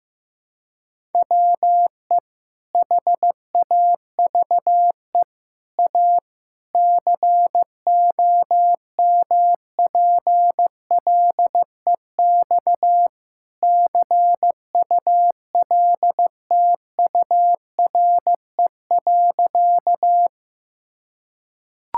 Morse code, heard as a keyed tone, about 16 wpm, 700 Hz. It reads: WE HAVE A COMPLEX CULTURE.
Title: Sentences from Top 1000 Words - Review Part 6 - 15wpm